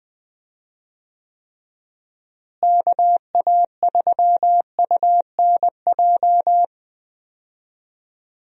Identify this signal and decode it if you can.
Morse code, heard as a keyed tone, about 20 wpm, 700 Hz. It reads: KA3UNJ
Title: USA Call Signs - 6 Characters - Repeated 3x - 20wpm